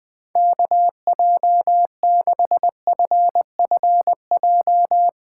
Morse code, heard as a keyed tone, about 20 wpm, 700 Hz. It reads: KJ6FFJ